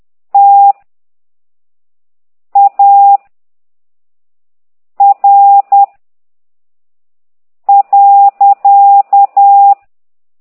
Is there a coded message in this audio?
Yes: Morse code from 0.3 to 9.7 s, heard as a keyed tone, about 10 wpm, 800 Hz. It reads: T A R .